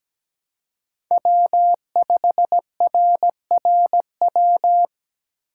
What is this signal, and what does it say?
Morse code, heard as a keyed tone, about 21 wpm, 700 Hz. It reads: W5RRW
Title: Speed Racing - USA Call Signs 17wpm